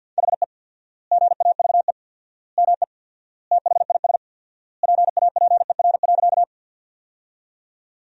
Morse code, heard as a keyed tone, about 49 wpm, 700 Hz. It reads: HE GAVE ME THIS PAPER.
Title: 4x Word-Spacing - Sentences from Top 500 Words - 50wpm